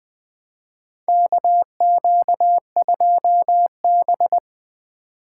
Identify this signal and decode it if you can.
Morse code, heard as a keyed tone, about 20 wpm, 700 Hz. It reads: KQ2B
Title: USA Call Signs - 4 Characters - 3x Character Spacing - 20wpm